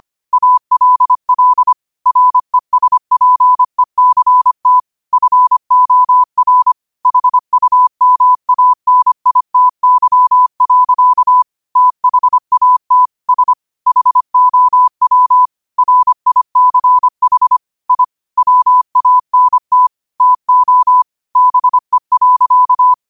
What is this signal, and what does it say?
Morse code, heard as a keyed tone, about 25 wpm, 1000 Hz. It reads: ALL RESPECT FOR HUMANITY. THAT S HOW RICH I WANT TO BE.